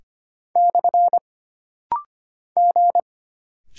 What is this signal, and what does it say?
Morse code, heard as a keyed tone, about 25 wpm, 700 Hz. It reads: / G
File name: Single Character (Letter:Number:Punctuation:Prosign) - Rapid-Fire - 25wpm